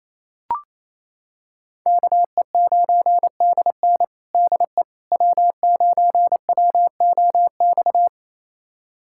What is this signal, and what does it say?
Morse code, heard as a keyed tone, about 28 wpm, 700 Hz. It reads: KE9DN DE W9WOX
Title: USA Callsign DE Callsign - 5 Characters - 28wpm